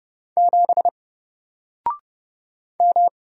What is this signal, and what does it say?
Morse code, heard as a keyed tone, about 30 wpm, 700 Hz. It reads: 7 M
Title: Single Character (Letter:Number:Punctuation:Prosign) 30wpm